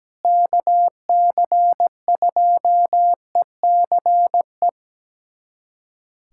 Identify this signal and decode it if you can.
Morse code, heard as a keyed tone, about 17 wpm, 700 Hz. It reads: KC2ECE
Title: USA Callsign DE Callsign 17wpm